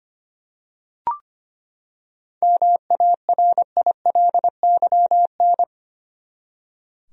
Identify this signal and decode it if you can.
Morse code, heard as a keyed tone, about 25 wpm, 700 Hz. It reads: MARILYN
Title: USA Common Names - 25wpm